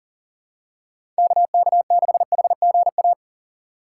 Morse code, heard as a keyed tone, about 40 wpm, 700 Hz. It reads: KK6HGA
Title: USA Call Signs - 6 Characters - 6x Character Spacing - 40wpm